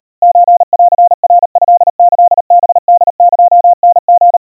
Morse code (Japanese Wabun code), heard as a keyed tone, about 39 wpm, 700 Hz. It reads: ソンナトキハホエタリ
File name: kawa / Morse sample / kamyu 140 cpm 700 Hz Crispy